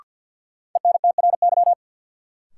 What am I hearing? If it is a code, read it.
Morse code, heard as a keyed tone, about 53 wpm, 700 Hz. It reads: ENTRY
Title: Sentences from Top 1500 Words - New Word Drill - 50wpm